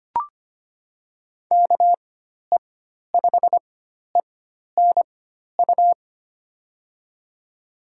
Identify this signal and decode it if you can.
Morse code, heard as a keyed tone, about 25 wpm, spaced as for 12 wpm, 700 Hz. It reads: KE5ENU
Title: USA Call Signs - 6 Characters - 4x Character Spacing - 25wpm